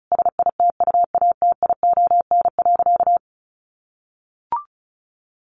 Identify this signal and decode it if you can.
Morse code, heard as a keyed tone, about 34 wpm, 700 Hz. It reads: SITUATION.
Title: Sentences from Top 1200 Words 35wpm